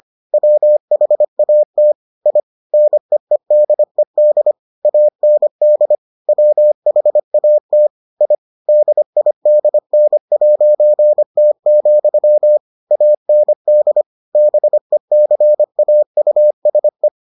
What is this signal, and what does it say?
Morse code, heard as a keyed tone, about 25 wpm, 600 Hz. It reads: WHAT I NEEDED AND WHAT I DIDN'T, AND BECAUSE